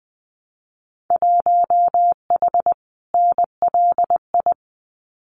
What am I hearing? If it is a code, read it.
Morse code, heard as a keyed tone, about 20 wpm, 700 Hz. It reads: 1H NLI
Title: Winter Field Day Exchanges 20wpm